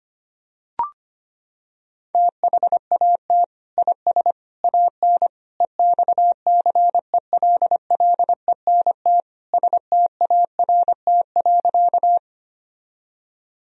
Morse code, heard as a keyed tone, about 25 wpm, 700 Hz. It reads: THAT IS AN EXCELLENT START.
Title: Sentences from Top 1000 Words - Review Part 1 - 25wpm